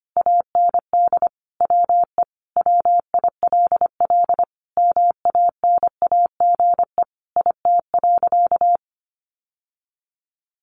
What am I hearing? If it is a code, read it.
Morse code, heard as a keyed tone, about 25 wpm, 700 Hz. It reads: AND WE WILL MANAGE IT.
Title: Sentences from Top 1400 Words 25wpm